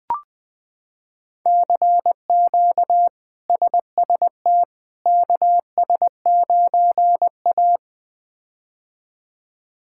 Morse code, heard as a keyed tone, about 20 wpm, 700 Hz. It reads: CQ SST KS9A
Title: K1USN SST QSO Practice - 20wpm